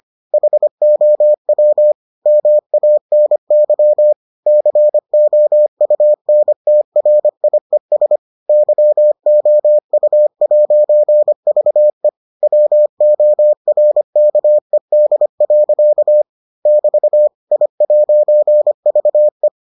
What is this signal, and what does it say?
Morse code, heard as a keyed tone, about 25 wpm, 600 Hz. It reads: HOW MANY COUNTRIES YOU'VE WORKED. <BT> I'VE